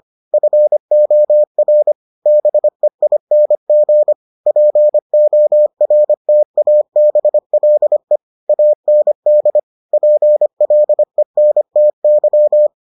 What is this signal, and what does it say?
Morse code, heard as a keyed tone, about 25 wpm, 600 Hz. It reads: FOR BEING PORTABLE AND PLENTY